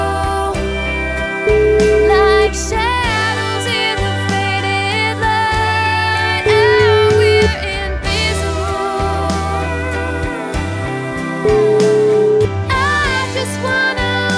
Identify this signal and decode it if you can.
Ringback tone (Europe, 425 Hz), with noise.